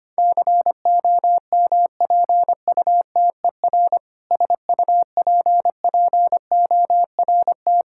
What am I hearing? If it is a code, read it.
Morse code, heard as a keyed tone, about 25 wpm, 700 Hz. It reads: COMPUTER SUPPORT
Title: K1USN SST QSO Practice 25wpm